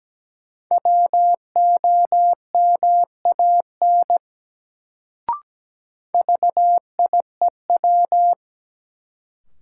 Morse code, heard as a keyed tone, about 17 wpm, 700 Hz. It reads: WOMAN VIEW
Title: Sets of 5 Words - New Word Drill - 17wpm